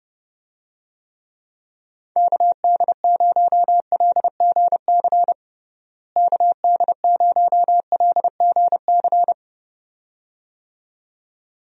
Morse code, heard as a keyed tone, about 30 wpm, 700 Hz. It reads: KD0LGC KD0LGC
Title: USA Call Signs - 6 Characters - Repeated 2x - 30wpm